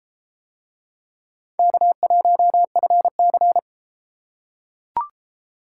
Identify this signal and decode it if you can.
Morse code, heard as a keyed tone, about 30 wpm, 700 Hz. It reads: K1FC